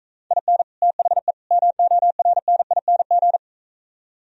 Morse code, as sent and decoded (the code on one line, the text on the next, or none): .. -. / - .... / -- --- .-. -. .. -. --.
IN THE MORNING